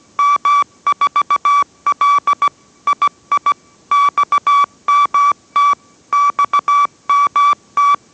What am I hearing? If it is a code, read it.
Morse code, heard as a keyed tone, about 17 wpm, 1150 Hz. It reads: M4L II XMT XMT